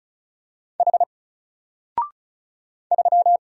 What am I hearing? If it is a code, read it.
Morse code, heard as a keyed tone, about 34 wpm, 700 Hz. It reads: H 3
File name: Single-Letter-Number-35wpm